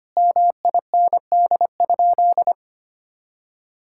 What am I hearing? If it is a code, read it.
Morse code, heard as a keyed tone, about 25 wpm, 700 Hz. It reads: MIND?
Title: Sentences from Top 1000 Words - Review Part 6 - 25wpm